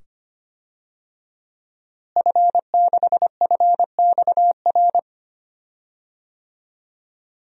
Morse code, heard as a keyed tone, about 25 wpm, 700 Hz. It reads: F6FXR
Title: DXCC-100 Call Signs - 5 Characters - Repeated 2x - 25wpm